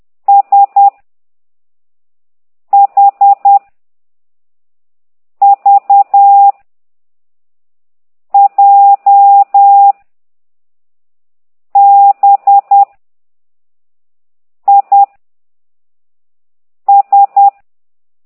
Morse code, heard as a keyed tone, about 10 wpm, 800 Hz. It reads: S H V J B I S
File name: ct2idw / exerc6